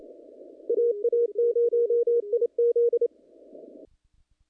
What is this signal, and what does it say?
Morse code, heard as a keyed tone, about 32 wpm, 470 Hz. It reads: AA0IZ
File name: contests / naqp-cw-k5tr-op-k5pi- -2100